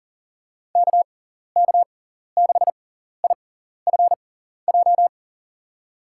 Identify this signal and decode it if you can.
Morse code, heard as a keyed tone, about 40 wpm, spaced as for 14 wpm, 700 Hz. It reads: KK6IFJ
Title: USA Call Signs - 6 Characters - 6x Character Spacing - 40wpm